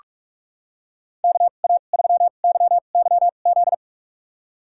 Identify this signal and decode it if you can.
Morse code, heard as a keyed tone, about 46 wpm, spaced as for 33 wpm, 700 Hz. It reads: KA3YYZ
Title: USA Call Signs - 6 Characters - 2x Character Spacing - 45wpm